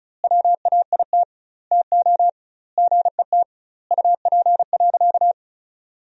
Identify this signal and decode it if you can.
Morse code, heard as a keyed tone, about 36 wpm, 700 Hz. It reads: WAIT TO GET UP.
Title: 2x Word-Spacing - Sentences from Top 500 Words - 35wpm